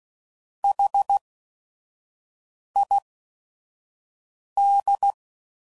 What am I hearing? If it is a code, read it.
Morse code, heard as a keyed tone, about 16 wpm, spaced as for 5 wpm, 800 Hz. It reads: HID